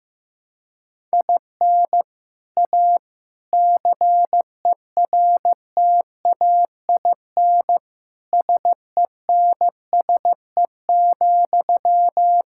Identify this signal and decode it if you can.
Morse code, heard as a keyed tone, about 15 wpm, 700 Hz. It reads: IN A CERTAIN SENSE,
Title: Sentences from Top 1000 Words - Review Part 6 - 15wpm